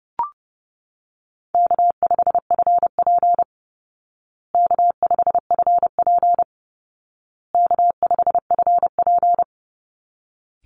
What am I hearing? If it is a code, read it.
Morse code, heard as a keyed tone, about 30 wpm, 700 Hz. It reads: K5FP K5FP K5FP
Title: Easy Call Signs Repeated 3x - 30wpm